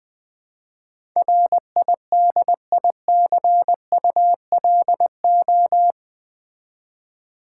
Morse code, heard as a keyed tone, about 20 wpm, 700 Hz. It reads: RIDICULO
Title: Build A Word - 4 Letters At A Time - 10-Letter Words - 20wpm